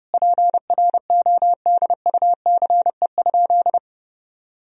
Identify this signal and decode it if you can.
Morse code, heard as a keyed tone, about 30 wpm, 700 Hz. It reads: PRODUCE?
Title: Sentences from Top 1000 Words - Review Part 3 - 30wpm